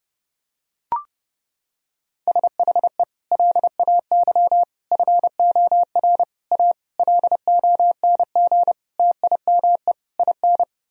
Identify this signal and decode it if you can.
Morse code, heard as a keyed tone, about 32 wpm, 700 Hz. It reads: SHE LAY FOR A LONG TIME IN
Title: Sentences from Top 1000 Words - Review Part 2 - 30wpm